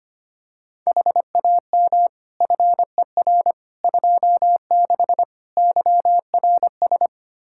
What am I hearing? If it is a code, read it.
Morse code, heard as a keyed tone, about 25 wpm, 700 Hz. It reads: HAM FER 26 YRS
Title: Practice QSO - Exchange 3 - More Info - 25wpm